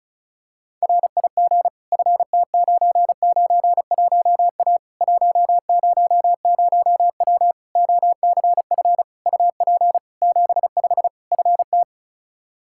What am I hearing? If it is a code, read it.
Morse code, heard as a keyed tone, about 36 wpm, 700 Hz. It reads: RIG FT991A 100W OCF UP 75 FT
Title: K1USN SST QSO Practice 35wpm